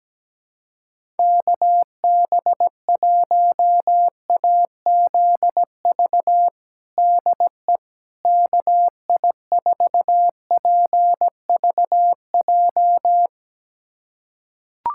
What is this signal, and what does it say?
Morse code, heard as a keyed tone, about 17 wpm, 700 Hz. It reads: KB1AZV DE KI4PVJ
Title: USA Callsign DE Callsign - 6 Characters - 17wpm